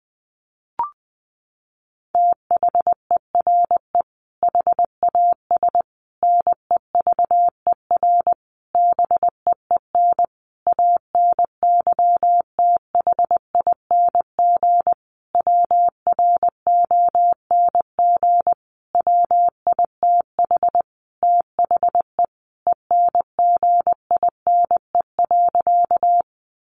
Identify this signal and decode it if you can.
Morse code, heard as a keyed tone, about 20 wpm, 700 Hz. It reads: THERE HAS NEVER BEEN ANYTHING WRONG WITH THE ENGINE.
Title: Sentences from Top 1500 Words 20wpm